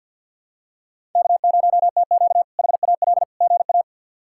Morse code, heard as a keyed tone, about 50 wpm, 700 Hz. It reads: K0TQ HAL GA
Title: K1USN SST QSO Practice - 50wpm